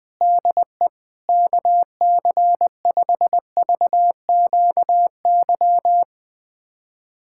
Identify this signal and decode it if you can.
Morse code, heard as a keyed tone, about 20 wpm, 700 Hz. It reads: DE KC5VQY
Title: USA Callsign DE Callsign - 6 Characters - 20wpm